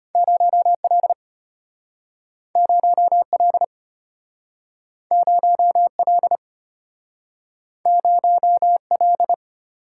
Morse code, heard as a keyed tone, about 34 wpm, 700 Hz. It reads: 0L 0L 0L 0L